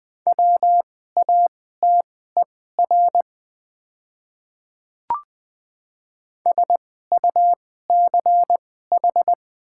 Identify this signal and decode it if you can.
Morse code, heard as a keyed tone, about 20 wpm, spaced as for 14 wpm, 700 Hz. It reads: WATER SUCH
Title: 2x Character Spacing - Top 200 Words - 20wpm